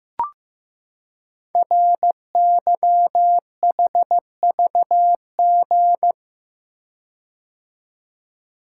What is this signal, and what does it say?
Morse code, heard as a keyed tone, about 15 wpm, 700 Hz. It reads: RYHVG
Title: Code Groups - 5 Letters - 15wpm